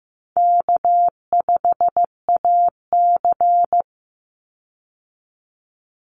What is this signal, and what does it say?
Morse code, heard as a keyed tone, about 15 wpm, 700 Hz. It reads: K5AC
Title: USA Call Signs 15wpm